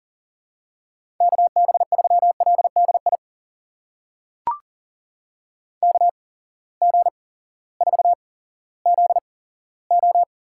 Morse code, heard as a keyed tone, about 40 wpm, spaced as for 18 wpm, 700 Hz. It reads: KB3LDI KG4ZO
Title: USA Call Signs - 6 Characters - 8x Character Spacing - 40wpm